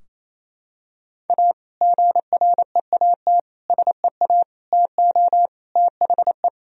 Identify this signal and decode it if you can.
Morse code, heard as a keyed tone, about 28 wpm, 700 Hz. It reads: A GREAT SEA TO THE
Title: Sentences from Top 1000 Words - Review Part 2 - 28wpm